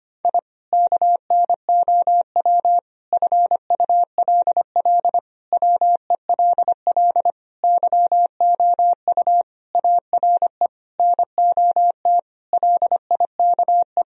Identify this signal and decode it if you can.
Morse code, heard as a keyed tone, about 25 wpm, 700 Hz. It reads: I KNOW FULL WELL YOU ARE NOT LIKE
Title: Sentences from Top 1000 Words - Review Part 3 - 25wpm